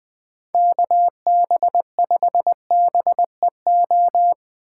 Morse code, heard as a keyed tone, about 22 wpm, 700 Hz. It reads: KB5BEO
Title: Speed Racing - USA 2x3 Call Signs 20wpm